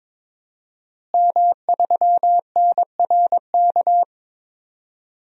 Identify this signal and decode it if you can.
Morse code, heard as a keyed tone, about 22 wpm, 700 Hz. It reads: M3NRK
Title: DXCC-100 Call Signs - 5 Characters - Repeated 3x - 22wpm